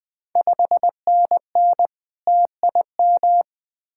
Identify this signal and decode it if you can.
Morse code, heard as a keyed tone, about 20 wpm, 700 Hz. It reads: 5NN TIM
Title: Ontario QSO Party Exchange 20wpm